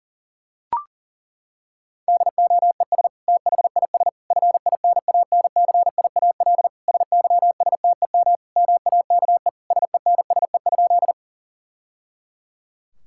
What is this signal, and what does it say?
Morse code, heard as a keyed tone, about 40 wpm, 700 Hz. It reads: DOES THIS FINANCIAL SYSTEM MAKE SENSE?
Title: Sentences from Top 1000 Words - Review Part 5 - 40wpm